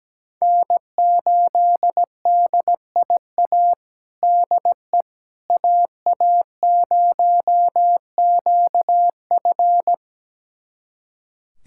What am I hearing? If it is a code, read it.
Morse code, heard as a keyed tone, about 17 wpm, 700 Hz. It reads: N8DIA DE AA0QF
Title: USA Callsign DE Callsign - 5 Characters - 17wpm